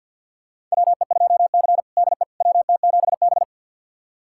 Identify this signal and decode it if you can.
Morse code, heard as a keyed tone, about 51 wpm, 700 Hz. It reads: WE2C DE WT7B
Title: USA Callsign DE Callsign - 4 Characters - 50wpm